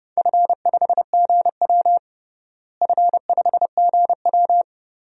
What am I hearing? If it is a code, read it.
Morse code, heard as a keyed tone, about 31 wpm, 700 Hz. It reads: F5GW F5GW
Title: DXCC-100 Call Signs - 4 Characters - Repeated 2x - 30wpm